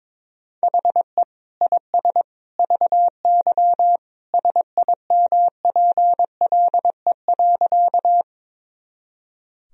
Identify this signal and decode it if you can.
Morse code, heard as a keyed tone, about 22 wpm, 700 Hz. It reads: HE IS VY SIMPLE.
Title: Sentences from Top 400 Words and Abbreviations 22wpm